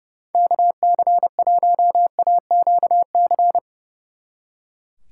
Morse code, heard as a keyed tone, about 36 wpm, 700 Hz. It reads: KC1AQC
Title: Speed Racing - USA Call Signs 30wpm